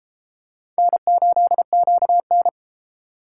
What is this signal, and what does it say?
Morse code, heard as a keyed tone, about 30 wpm, 700 Hz. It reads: N8QN